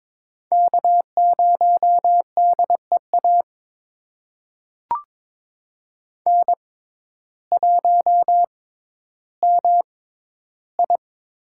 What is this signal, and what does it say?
Morse code, heard as a keyed tone, about 21 wpm, spaced as for 12 wpm, 700 Hz. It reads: K0DEA N1MI
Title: USA Call Signs - 6x Character Spacing - 22wpm